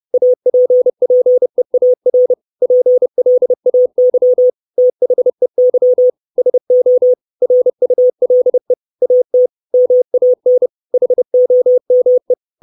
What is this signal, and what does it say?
Morse code, heard as a keyed tone, about 30 wpm, 500 Hz. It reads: APPEAR PLAY THEY SO RULE AT MAN HOME
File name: 632 Words 30WPM